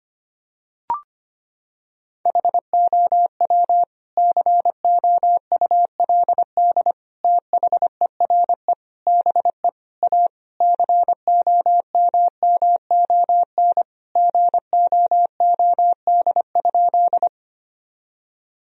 Morse code, heard as a keyed tone, about 25 wpm, 700 Hz. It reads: HOW COULD THERE BE A COMMON GOOD?
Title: Sentences from Top 1000 Words - Review Part 3 - 25wpm